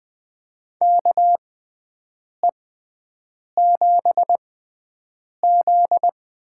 Morse code, heard as a keyed tone, about 20 wpm, spaced as for 7 wpm, 700 Hz. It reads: KE7Z
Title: USA Call Signs - 6 Characters - 6x Character Spacing - 20wpm